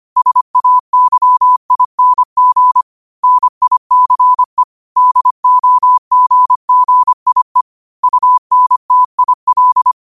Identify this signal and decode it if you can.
Morse code, heard as a keyed tone, about 25 wpm, 1000 Hz. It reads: SAYING NICE DOGGIE UNTIL